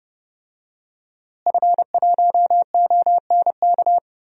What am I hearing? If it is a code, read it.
Morse code, heard as a keyed tone, about 30 wpm, 700 Hz. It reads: F1ONK
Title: DXCC-100 Call Signs - 5 Characters - Repeated 3x - 30wpm